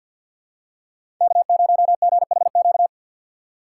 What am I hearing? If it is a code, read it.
Morse code, heard as a keyed tone, about 53 wpm, 700 Hz. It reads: K0GHQ
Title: USA Call Signs - 5 Characters - 2x Character Spacing - 50wpm